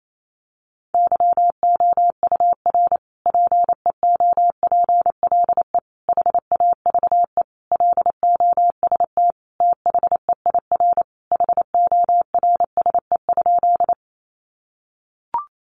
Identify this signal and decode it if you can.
Morse code, heard as a keyed tone, about 28 wpm, 700 Hz. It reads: YOUR PEOPLE HAVE LOST THEIR HORSE?
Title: Sentences from Top 1400 Words 28wpm